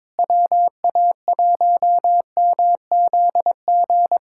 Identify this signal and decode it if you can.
Morse code, heard as a keyed tone, about 22 wpm, 700 Hz. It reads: WA1MZG
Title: USA Callsign DE Callsign 22wpm